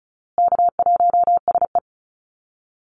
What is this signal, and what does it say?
Morse code, heard as a keyed tone, about 34 wpm, 700 Hz. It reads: K1SE
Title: Easy Call Signs Repeated 3x - 35wpm